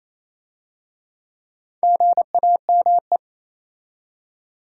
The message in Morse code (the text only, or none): GAME